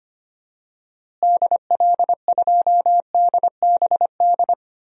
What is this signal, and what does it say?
Morse code, heard as a keyed tone, about 25 wpm, 700 Hz. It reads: DL2DBD
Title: DXCC-100 Call Signs - 6 Characters - Repeated 2x - 25wpm